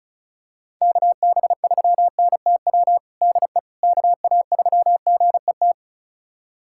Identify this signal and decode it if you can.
Morse code, heard as a keyed tone, about 36 wpm, 700 Hz. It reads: KB3NTW DE KA3GET
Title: USA Callsign DE Callsign - 6 Characters - 35wpm